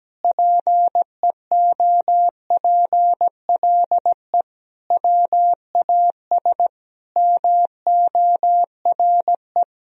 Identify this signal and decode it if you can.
Morse code, heard as a keyed tone, about 17 wpm, 700 Hz. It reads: PEOPLE WAS MORE
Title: Sentences from Top 1000 Words - Review Part 2 - 17wpm